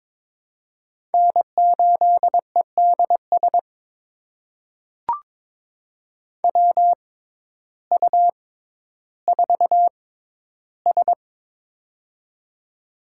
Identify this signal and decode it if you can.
Morse code, heard as a keyed tone, about 22 wpm, spaced as for 12 wpm, 700 Hz. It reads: N8EDS WU4S
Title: USA Call Signs - 6x Character Spacing - 22wpm